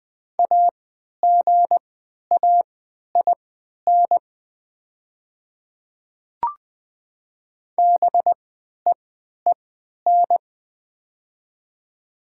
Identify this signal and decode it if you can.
Morse code, heard as a keyed tone, about 20 wpm, spaced as for 11 wpm, 700 Hz. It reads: AGAIN BEEN
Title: 3x Character Spacing - Top 200 Words - 20wpm